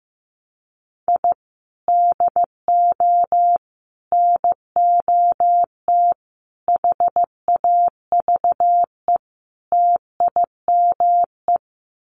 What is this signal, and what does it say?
Morse code, heard as a keyed tone, about 15 wpm, 700 Hz. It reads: I DO NOT HAVE TIME
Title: Sets of 5 Words Encore 15wpm